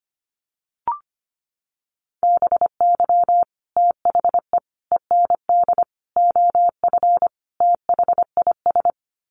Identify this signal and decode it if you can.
Morse code, heard as a keyed tone, about 25 wpm, 700 Hz. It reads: BY THE END OF THIS
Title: Sets of 5 Words Encore 25wpm